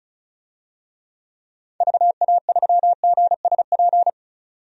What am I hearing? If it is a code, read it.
Morse code, heard as a keyed tone, about 34 wpm, 700 Hz. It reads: VA3GSP
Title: DXCC-100 Call Signs - 6 Characters - 35wpm